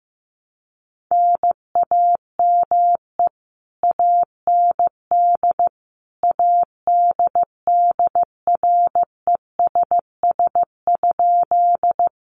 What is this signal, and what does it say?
Morse code, heard as a keyed tone, about 15 wpm, 700 Hz. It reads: NAME AND ADDRESS?
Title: Sentences from Top 1000 Words 15wpm